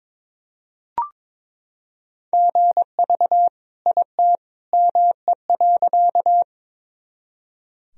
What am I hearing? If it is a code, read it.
Morse code, heard as a keyed tone, about 21 wpm, 700 Hz. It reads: GV IT ME.
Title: Sentences from Top 200 Words and Abbreviations 22wpm